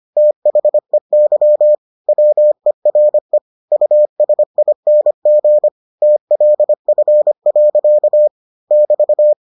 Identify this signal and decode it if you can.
Morse code, heard as a keyed tone, about 25 wpm, 600 Hz. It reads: THEY WERE USING TLF. <BT>